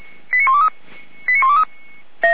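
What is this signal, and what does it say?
Five-tone selective call (ZVEI), heard as quick runs of five tones, tones of about 80 ms. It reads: 89123 89123
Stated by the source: ZVEI five-tone selective call received with an SDR.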